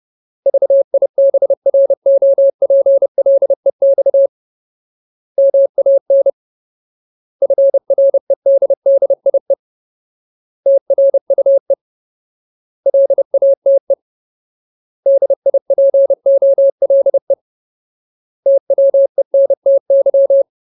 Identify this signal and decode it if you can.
Morse code, heard as a keyed tone, about 30 wpm, 550 Hz. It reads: VIBROPLEX MAN FREDDIE TRUE LATE DIPOLE TWENTY
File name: Ultimate Word List / 30WPM Ultimate